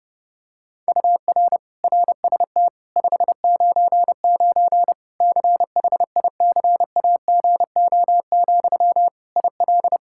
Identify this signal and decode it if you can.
Morse code, heard as a keyed tone, about 32 wpm, 700 Hz. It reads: UR RST 599 CHICAGO, IL
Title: Practice QSO - Exchange 1 - Beginning - 30wpm